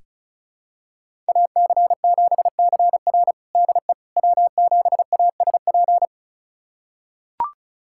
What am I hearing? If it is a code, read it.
Morse code, heard as a keyed tone, about 36 wpm, 700 Hz. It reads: AC7CR DE W7ASP
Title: USA Callsign DE Callsign - 5 Characters - 35wpm